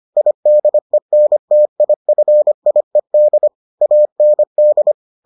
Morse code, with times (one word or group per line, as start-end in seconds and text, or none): 0.2-3.5 IDENTIFIED
3.8-4.9 AND